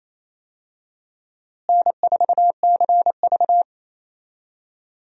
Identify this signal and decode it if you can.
Morse code, heard as a keyed tone, about 28 wpm, 700 Hz. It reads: N4CV
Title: USA Call Signs - 4 Characters - Repeated 3x - 28wpm